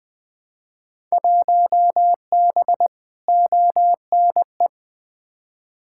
Morse code, heard as a keyed tone, about 20 wpm, 700 Hz. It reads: 1B ONE
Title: Field Day Exchanges - 20wpm